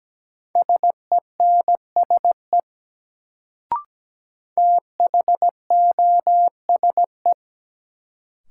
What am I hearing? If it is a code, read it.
Morse code, heard as a keyed tone, about 17 wpm, 700 Hz. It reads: SENSE THOSE
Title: Sets of 2 Words - New Word Drill - 17wpm